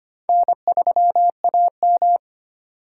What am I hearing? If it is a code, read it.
Morse code, heard as a keyed tone, about 32 wpm, 700 Hz. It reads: N3AM